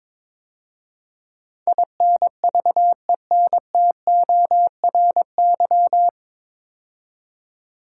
Morse code, heard as a keyed tone, about 22 wpm, 700 Hz. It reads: INVENTORY
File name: 9 Letter Words - First 3 Letters Given - 22wpm